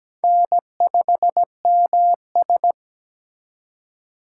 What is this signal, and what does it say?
Morse code, heard as a keyed tone, about 20 wpm, 700 Hz. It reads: N5MS
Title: Speed Racing - USA 1x2 Call Signs 17wpm